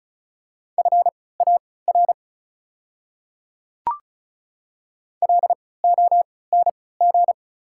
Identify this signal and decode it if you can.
Morse code, heard as a keyed tone, about 36 wpm, spaced as for 20 wpm, 690 Hz. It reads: FAR LONG
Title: 3x Character Spacing - Top 300 Words - 35wpm